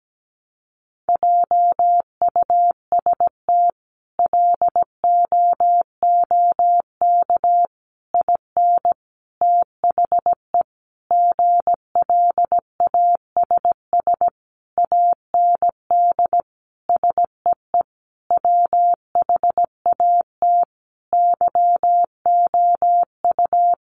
Morse code, heard as a keyed tone, about 17 wpm, 700 Hz. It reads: JUST LOOK IN THE GLASS AND SEE WHAT YOU
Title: Sentences from Top 1300 Words 17wpm